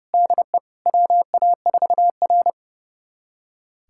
Morse code, heard as a keyed tone, about 32 wpm, 700 Hz. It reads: DE WA4R
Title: USA Callsign DE Callsign - 4 Characters - 30wpm